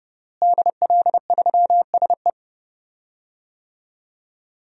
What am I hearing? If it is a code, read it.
Morse code, heard as a keyed tone, about 30 wpm, 700 Hz. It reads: DL3SE
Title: DXCC-100 Call Signs - 5 Characters - Repeated 2x - 30wpm